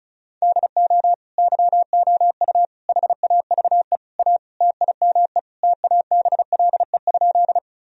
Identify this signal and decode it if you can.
Morse code, heard as a keyed tone, about 36 wpm, 700 Hz. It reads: DO YOU HAVE A TIME TABLE?
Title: Sentences from Top 1000 Words - Review Part 2 - 35wpm